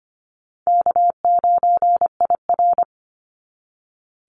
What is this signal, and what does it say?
Morse code, heard as a keyed tone, about 25 wpm, 700 Hz. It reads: K9IR
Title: NA SOTA Activator Callsigns 25wpm